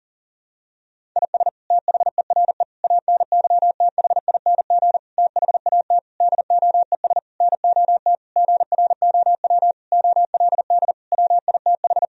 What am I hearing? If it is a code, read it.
Morse code, heard as a keyed tone, about 40 wpm, 700 Hz. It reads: IS THERE ANYTHING THAT DOES NOT GROW OLD WITH
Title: Sentences from Top 1000 Words - Review Part 1 - 40wpm